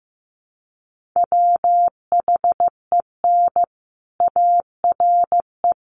Morse code, heard as a keyed tone, about 15 wpm, 700 Hz. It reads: WHEN ARE
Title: Sets of 5 Words Encore2 15wpm